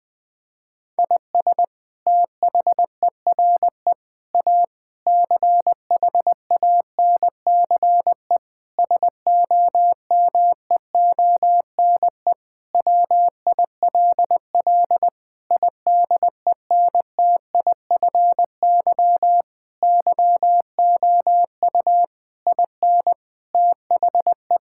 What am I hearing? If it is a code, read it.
Morse code, heard as a keyed tone, about 21 wpm, 700 Hz. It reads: IS THERE A CHANCE SOMEONE WILL IDENTIFY YOU IN THE
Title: Sentences from Top 1000 Words - Review Part 6 - 20wpm